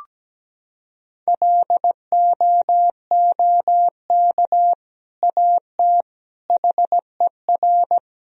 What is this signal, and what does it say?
Morse code, heard as a keyed tone, about 17 wpm, 700 Hz. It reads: LOOK AT HER
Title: Sentences from Top 1000 Words - Review Part 1 - 17wpm